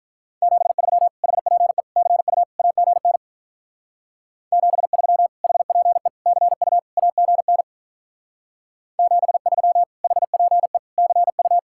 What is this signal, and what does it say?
Morse code, heard as a keyed tone, about 44 wpm, 700 Hz. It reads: 73 HPE CU AGN 73 HPE CU AGN 73 HPE CU
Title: Speed-Racing Common QSO Phrases - 35wpm